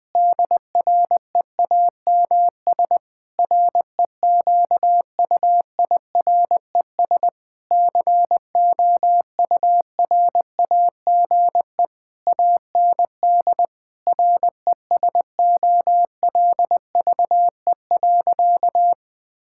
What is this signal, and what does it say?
Morse code, heard as a keyed tone, about 20 wpm, 700 Hz. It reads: DREAMS REQUIRES COURAGE AND RESOLVE.
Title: Difficult Sentences with Hints - 20wpm